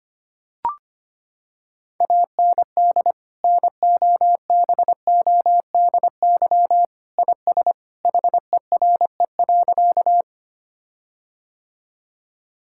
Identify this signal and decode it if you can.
Morse code, heard as a keyed tone, about 25 wpm, 700 Hz. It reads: AND NOBODY IS HERE.
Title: Sentences from Top 1600 Words 25wpm